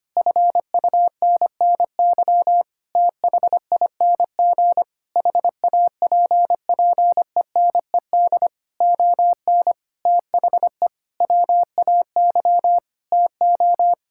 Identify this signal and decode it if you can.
Morse code, heard as a keyed tone, about 25 wpm, 700 Hz. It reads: FUNNY THING HAPPENED ON THE WAY TO
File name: Sets of 9 Words - 25wpm